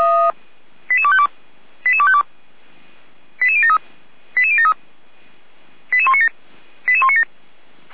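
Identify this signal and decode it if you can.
Five-tone selective call (ZVEI), heard as quick runs of five tones, tones of about 70 ms. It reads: 88252 88252 88073 88073 88197 88197
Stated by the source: ZVEI five-tone selective call received by SDR, from the Feuerwehr Orlamünde fire brigade audio server.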